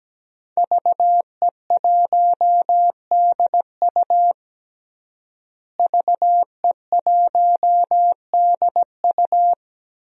Morse code, heard as a keyed tone, about 17 wpm, 700 Hz. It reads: VE1DU VE1DU
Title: DXCC-100 Call Signs - 5 Characters - Repeated 2x - 17wpm